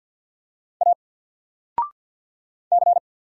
Morse code, heard as a keyed tone, about 50 wpm, 700 Hz. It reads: A C
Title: Single Letter-Number Alt 50wpm